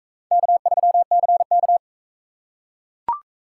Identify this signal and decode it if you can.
Morse code, heard as a keyed tone, about 37 wpm, 700 Hz. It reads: K3CK